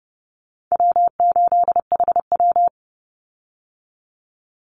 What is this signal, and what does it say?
Morse code, heard as a keyed tone, about 30 wpm, 700 Hz. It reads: W8HW
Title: Call Signs - 30wpm